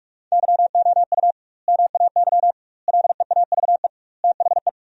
Morse code, heard as a keyed tone, about 46 wpm, 700 Hz. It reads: YOU MAY LEAVE THE
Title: 2x Word-Spacing - Sentences from Top 400 Words - 45wpm